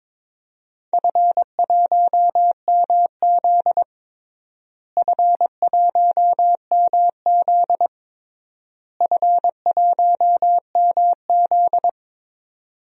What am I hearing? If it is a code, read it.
Morse code, heard as a keyed tone, about 21 wpm, 700 Hz. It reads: F1MZ F1MZ F1MZ
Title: DXCC-100 Call Signs - 4 Characters - Repeated 3x - 22wpm